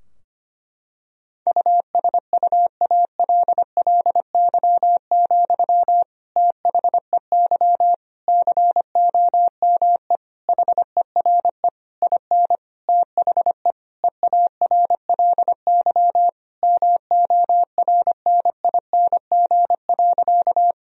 Morse code, heard as a keyed tone, about 25 wpm, 700 Hz. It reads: USUALLY, THEY COME HERE IN THE EARLY MORNING.
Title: Sentences from Top 1000 Words - Review Part 3 - 25wpm